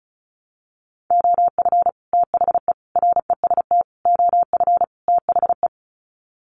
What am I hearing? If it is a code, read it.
Morse code, heard as a keyed tone, about 34 wpm, 700 Hz. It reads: OF THE REST OF THE
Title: Sets of 5 Words 35wpm